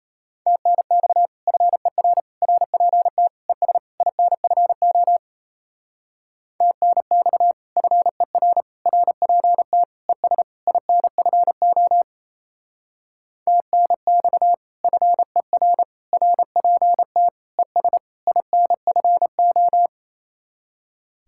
Morse code, heard as a keyed tone, about 34 wpm, 700 Hz. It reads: TNX FER RPT ES INFO TNX FER RPT ES INFO TNX FER RPT ES INFO
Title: Speed-Racing Common QSO Phrases - 28wpm